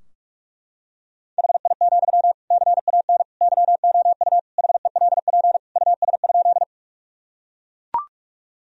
Morse code, heard as a keyed tone, about 43 wpm, 700 Hz. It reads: HI, CAN YOU HELP US?
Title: Sentences from Top 1000 Words - Review Part 3 - 45wpm